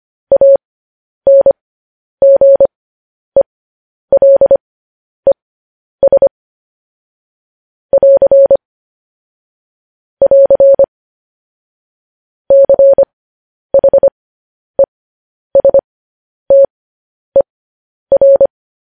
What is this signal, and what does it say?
Morse code, heard as a keyed tone, about 25 wpm, spaced as for 10 wpm, 550 Hz. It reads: ANGELES <AR> <AR> CHESTER
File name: podcast / News headlines-1x-25x10